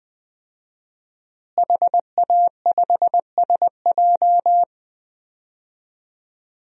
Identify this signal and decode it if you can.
Morse code, heard as a keyed tone, about 20 wpm, 700 Hz. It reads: HA5SJ
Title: DXCC-100 Call Signs - 5 Characters - Repeated 3x - 20wpm